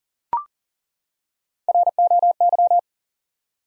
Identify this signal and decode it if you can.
Morse code, heard as a keyed tone, about 37 wpm, 700 Hz. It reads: ROY
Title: USA Common Names - 40wpm